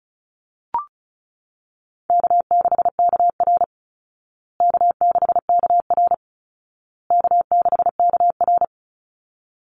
Morse code, heard as a keyed tone, about 34 wpm, 700 Hz. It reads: K6KR K6KR K6KR
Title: Easy Call Signs Repeated 3x - 35wpm